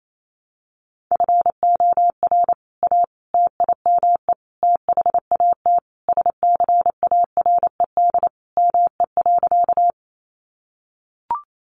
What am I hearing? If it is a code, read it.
Morse code, heard as a keyed tone, about 28 wpm, 700 Hz. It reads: FOR A TIME THAT SCARED ME.
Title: Sentences from Top 700 Words 28wpm